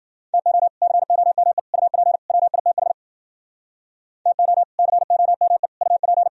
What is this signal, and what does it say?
Morse code, heard as a keyed tone, about 51 wpm, 700 Hz. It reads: TO COME UP WITH TO COME UP